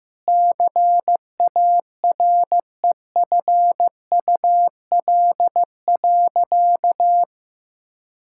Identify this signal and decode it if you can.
Morse code, heard as a keyed tone, about 16 wpm, 700 Hz. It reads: CAREFUL.
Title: Sentences from Top 1600 Words 15wpm